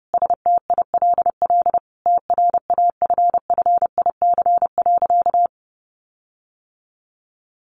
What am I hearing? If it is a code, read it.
Morse code, heard as a keyed tone, about 30 wpm, 700 Hz. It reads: STILL TRAFFIC.
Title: Sentences from Top 1300 Words 30wpm